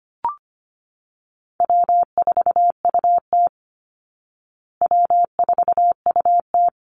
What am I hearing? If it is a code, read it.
Morse code, heard as a keyed tone, about 25 wpm, 700 Hz. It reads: W4UT W4UT
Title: Call Signs Repeated 3x - 25wpm